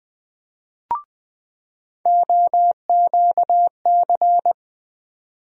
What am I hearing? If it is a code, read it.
Morse code, heard as a keyed tone, about 20 wpm, 700 Hz. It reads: OQC
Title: USA Call Sign Postfixes - 3 Letters 20wpm